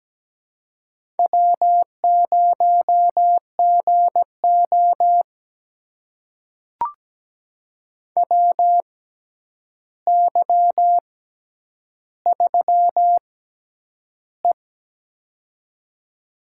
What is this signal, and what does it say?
Morse code, heard as a keyed tone, about 17 wpm, spaced as for 9 wpm, 700 Hz. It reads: W0GO WY3E
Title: USA Call Signs - 4 Characters - 6x Character Spacing - 17wpm